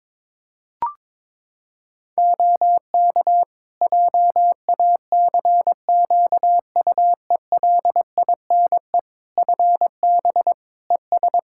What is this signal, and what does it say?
Morse code, heard as a keyed tone, about 22 wpm, 700 Hz. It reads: OK JACQUELINE FB ES
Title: K1USN SST QSO Practice 22wpm